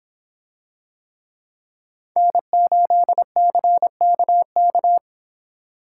Morse code, heard as a keyed tone, about 26 wpm, 700 Hz. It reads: N8CKK